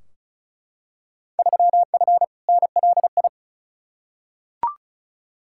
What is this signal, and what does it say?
Morse code, heard as a keyed tone, about 36 wpm, 700 Hz. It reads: 3F NLI
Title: Field Day Exchanges Encore - 35wpm